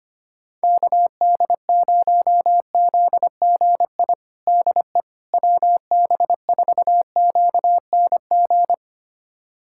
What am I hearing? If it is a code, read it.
Morse code, heard as a keyed tone, about 25 wpm, 700 Hz. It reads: KD0ZGI DE WB4QNG
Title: USA Callsign DE Callsign - 6 Characters - 25wpm